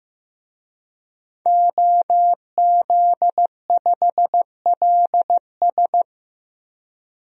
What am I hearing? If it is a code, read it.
Morse code, heard as a keyed tone, about 15 wpm, 700 Hz. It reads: OZ5LS
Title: DXCC-100 Call Signs - 5 Characters - Repeated 3x - 15wpm